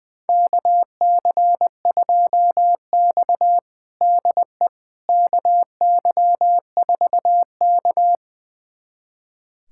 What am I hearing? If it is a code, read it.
Morse code, heard as a keyed tone, about 21 wpm, 700 Hz. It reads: KC2X DE KY4K
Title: USA Callsign DE Callsign - 4 Characters - 20wpm